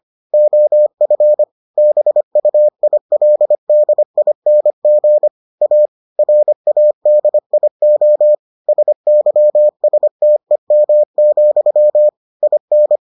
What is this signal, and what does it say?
Morse code, heard as a keyed tone, about 25 wpm, 600 Hz. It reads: OF BUILDING A RADIO SYSTEM, IN